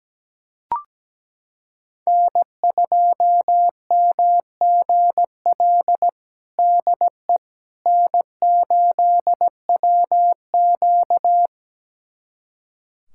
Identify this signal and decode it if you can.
Morse code, heard as a keyed tone, about 17 wpm, 700 Hz. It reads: N2MGL DE N8WQ
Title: USA Callsign DE Callsign 17wpm